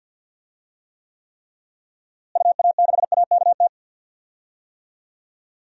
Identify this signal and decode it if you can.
Morse code, heard as a keyed tone, about 52 wpm, 700 Hz. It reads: UA6AKT
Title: DXCC-100 Call Signs - 6 Characters - Repeated 3x - 50wpm